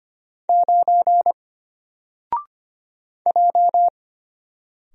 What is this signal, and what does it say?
Morse code, heard as a keyed tone, about 25 wpm, 700 Hz. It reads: 9 J
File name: Single-Letter-Number-25wpm